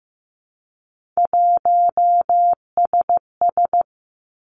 Morse code, heard as a keyed tone, about 15 wpm, 700 Hz. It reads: 1SS
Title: Three Challenging Letter-Number Combinations 15wpm